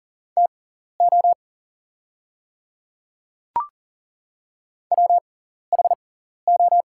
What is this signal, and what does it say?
Morse code, heard as a keyed tone, about 37 wpm, spaced as for 14 wpm, 700 Hz. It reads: TO WHO
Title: 6x Character Spacing - Top 100 Words - 40wpm